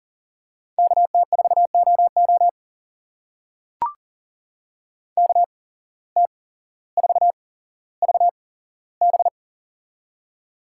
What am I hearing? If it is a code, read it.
Morse code, heard as a keyed tone, about 40 wpm, spaced as for 17 wpm, 700 Hz. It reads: KT4OO KT4VB
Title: USA Call Signs - 5 Characters - 8x Character Spacing - 40wpm